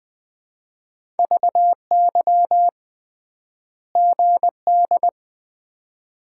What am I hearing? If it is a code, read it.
Morse code, heard as a keyed tone, about 20 wpm, 700 Hz. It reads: VY GD
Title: Common QSO Phrases - 3x Word-Spacing - 20wpm